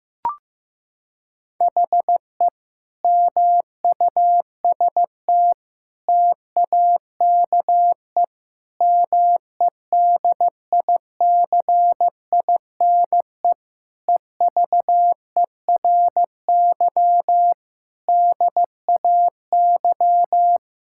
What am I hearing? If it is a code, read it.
Morse code, heard as a keyed tone, about 15 wpm, 700 Hz. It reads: HE MUST TAKE MEDICINE EVERY DAY
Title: Sentences from Top 1000 Words - Review Part 6 - 15wpm